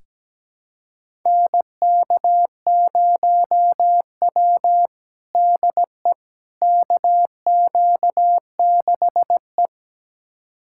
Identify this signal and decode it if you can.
Morse code, heard as a keyed tone, about 17 wpm, 700 Hz. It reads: NK0W DE KQ6E
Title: USA Callsign DE Callsign - 4 Characters - 17wpm